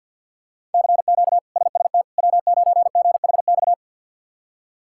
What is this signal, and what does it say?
Morse code, heard as a keyed tone, about 53 wpm, 690 Hz. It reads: CQ SST W9GHX
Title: K1USN SST QSO Practice - 50wpm